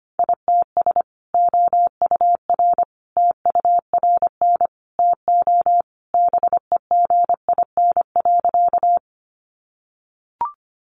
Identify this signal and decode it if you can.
Morse code, heard as a keyed tone, about 25 wpm, 700 Hz. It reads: ITS OUR TURN TO BEGIN.
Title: Sentences from Top 600 Words 25wpm